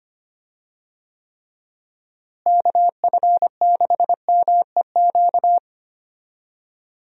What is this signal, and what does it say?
Morse code, heard as a keyed tone, about 25 wpm, 700 Hz. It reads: KF6MEQ
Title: USA Call Signs - 6 Characters - Repeated 2x - 25wpm